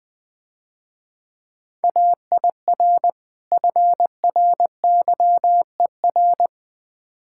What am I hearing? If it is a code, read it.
Morse code, heard as a keyed tone, about 21 wpm, 700 Hz. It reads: AIR FRYER
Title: Difficult Words - Speed-Racing with Hints - 20wpm